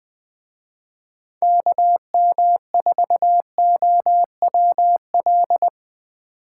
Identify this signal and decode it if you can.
Morse code, heard as a keyed tone, about 23 wpm, 700 Hz. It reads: KM4OWL
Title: Speed Racing - USA Call Signs 20wpm